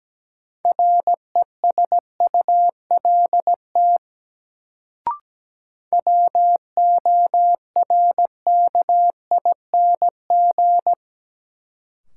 Morse code, heard as a keyed tone, about 17 wpm, 700 Hz. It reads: RESULT WORKING
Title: Sentences from Top 700 Words - New Word Drill - 17wpm